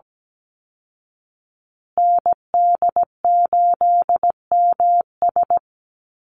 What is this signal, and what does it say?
Morse code, heard as a keyed tone, about 17 wpm, 700 Hz. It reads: ND8MS
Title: Call Signs Repeated 2x - 17wpm